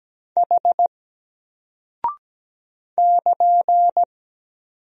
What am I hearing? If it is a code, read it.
Morse code, heard as a keyed tone, about 17 wpm, 700 Hz. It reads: H <KN>
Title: Single Character (Letter:Number:Punctuation:Prosign) 17wpm